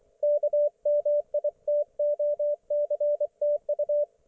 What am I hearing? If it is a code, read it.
Morse code, heard as a keyed tone, about 24 wpm, 590 Hz, with radio noise. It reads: KMITOCTU
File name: Archiv / CUC212 / ok1fhi cul212